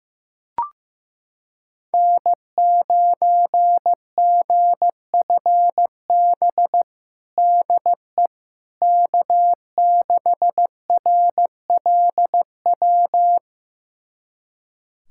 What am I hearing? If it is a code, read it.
Morse code, heard as a keyed tone, about 15 wpm, 700 Hz. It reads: N9GFB DE K6RLW
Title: USA Callsign DE Callsign - 5 Characters - 15wpm